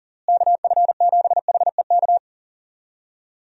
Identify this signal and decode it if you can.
Morse code, heard as a keyed tone, about 40 wpm, 700 Hz. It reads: KF7HEK
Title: USA Call Signs - 6 Characters - Repeated 3x - 40wpm